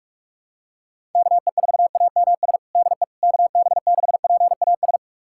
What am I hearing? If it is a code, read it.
Morse code, heard as a keyed tone, about 46 wpm, 700 Hz. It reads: KE4AMS DE KB6PAS
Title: USA Callsign DE Callsign 45wpm